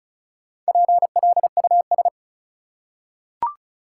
Morse code, heard as a keyed tone, about 34 wpm, 700 Hz. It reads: PLUS
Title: Four Letter Words - 35wpm